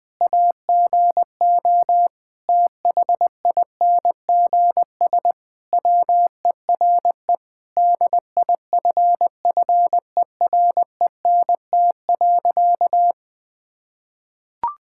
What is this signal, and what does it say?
Morse code, heard as a keyed tone, about 20 wpm, 700 Hz. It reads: AGO THINGS WERE DIFFERENT.
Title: Sentences with 4-Word Phrases - 20wpm